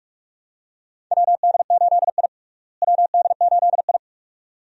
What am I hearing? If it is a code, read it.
Morse code, heard as a keyed tone, about 42 wpm, 700 Hz. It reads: WD8I WD8I
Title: USA Call Signs - 4 Characters - Repeated 2x - 45wpm